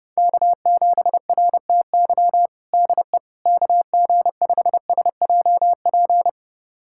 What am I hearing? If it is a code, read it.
Morse code, heard as a keyed tone, about 32 wpm, 700 Hz. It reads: K7RTY DE KG5SJP
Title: USA Callsign DE Callsign 30wpm